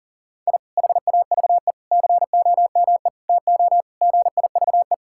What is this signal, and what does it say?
Morse code, heard as a keyed tone, about 40 wpm, 690 Hz. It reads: I HAVE COME TO GIVE
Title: Sentences from Top 1000 Words - Review Part 6 - 40wpm